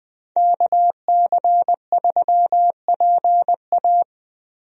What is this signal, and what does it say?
Morse code, heard as a keyed tone, about 20 wpm, 700 Hz. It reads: KC3PA
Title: USA Call Signs - 5 Characters - Repeated 3x - 20wpm